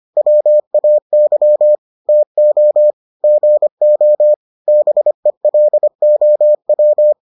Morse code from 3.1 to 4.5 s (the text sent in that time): GO